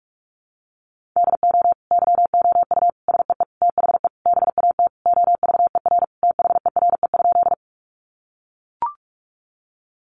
Morse code, heard as a keyed tone, about 45 wpm, 700 Hz. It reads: DO YOU SEE THE BAT OVER THERE?
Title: Sentences from Top 1500 Words 45wpm